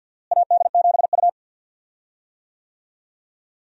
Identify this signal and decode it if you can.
Morse code, heard as a keyed tone, about 52 wpm, 700 Hz. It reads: AD7U
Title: USA Call Signs - 4 Characters - Repeated 3x - 50wpm